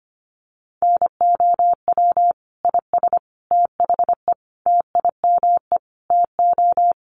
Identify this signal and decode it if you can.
Morse code, heard as a keyed tone, about 25 wpm, 700 Hz. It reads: NOW IS THE TIME TO
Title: Sets of 5 Words Encore2 25wpm